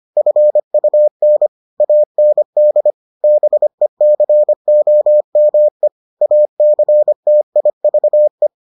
Morse code, heard as a keyed tone, about 25 wpm, 600 Hz. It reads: FUN AND BECOME ACTIVE